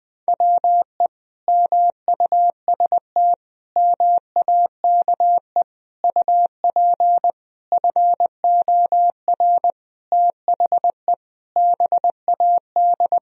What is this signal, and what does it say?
Morse code, heard as a keyed tone, about 20 wpm, 700 Hz. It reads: WE MUST MAKE UP FOR THE BAD
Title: Sentences from Top 1000 Words - Review Part 3 - 20wpm